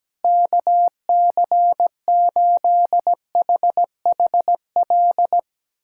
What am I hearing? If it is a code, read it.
Morse code, heard as a keyed tone, about 17 wpm, 700 Hz. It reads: KC8HHL
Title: USA Callsign DE Callsign 17wpm